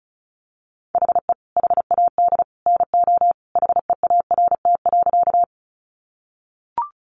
Morse code, heard as a keyed tone, about 34 wpm, 700 Hz. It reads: HE HAD NO HEART.
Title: Sentences from Top 1200 Words 35wpm